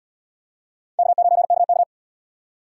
Morse code, heard as a keyed tone, about 67 wpm, 700 Hz. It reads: K9KK